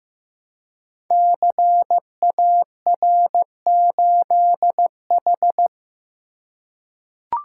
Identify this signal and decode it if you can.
Morse code, heard as a keyed tone, about 15 wpm, 700 Hz. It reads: CAR8H
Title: Code Groups - 5 Letter:Numbers - 15wpm